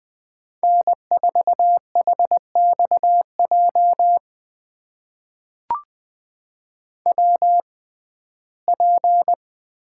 Morse code, heard as a keyed tone, about 20 wpm, spaced as for 14 wpm, 700 Hz. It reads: N4HXJ WP
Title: USA Call Signs - 5 Characters - 6x Character Spacing - 20wpm